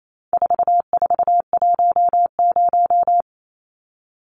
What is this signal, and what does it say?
Morse code, heard as a keyed tone, about 28 wpm, 700 Hz. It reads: 4410
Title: Numbers - 4-Digit 28wpm